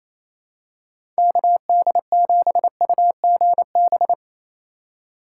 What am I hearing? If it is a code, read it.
Morse code, heard as a keyed tone, about 28 wpm, 700 Hz. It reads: KD7UGB
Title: USA Call Signs - 6 Characters - 2x Character Spacing - 28wpm